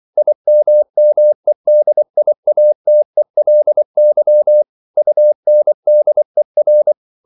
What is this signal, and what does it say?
Morse code, heard as a keyed tone, about 24 wpm, 600 Hz. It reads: IMMEDIATELY UNDER